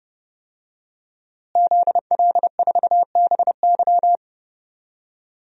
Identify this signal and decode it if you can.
Morse code, heard as a keyed tone, about 30 wpm, 700 Hz. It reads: ZL4BY
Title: DXCC-100 Call Signs - 5 Characters - Repeated 2x - 30wpm